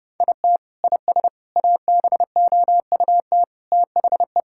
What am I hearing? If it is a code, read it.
Morse code, heard as a keyed tone, about 30 wpm, 700 Hz. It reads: IT IS ABOUT THE
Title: Sentences from Top 1000 Words - Review Part 6 - 30wpm